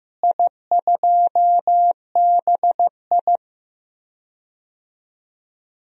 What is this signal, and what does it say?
Morse code, heard as a keyed tone, about 15 wpm, 700 Hz. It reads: I2BI
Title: DXCC-100 Call Signs - 4 Characters - Repeated 2x - 15wpm